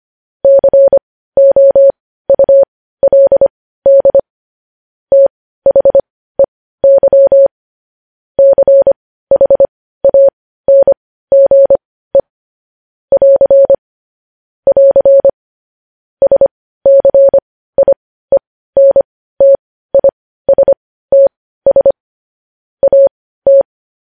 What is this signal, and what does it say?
Morse code, heard as a keyed tone, about 25 wpm, spaced as for 15 wpm, 550 Hz. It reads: COULD THEY CHANGE <AR> <AR> SCIENTISTS AT
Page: news headlines in morse code at 15 wpm